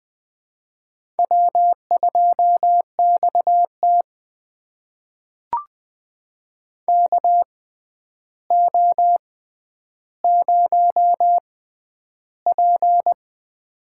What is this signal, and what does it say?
Morse code, heard as a keyed tone, about 20 wpm, spaced as for 10 wpm, 700 Hz. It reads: W2XT KO0P
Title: USA Call Signs - 4 Characters - 6x Character Spacing - 20wpm